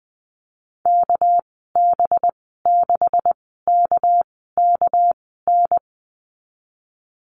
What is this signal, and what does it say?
Morse code, heard as a keyed tone, about 20 wpm, spaced as for 15 wpm, 700 Hz. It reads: KB6KKN
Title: 207 Lesson 33g - Letter j - Call Signs 20wpm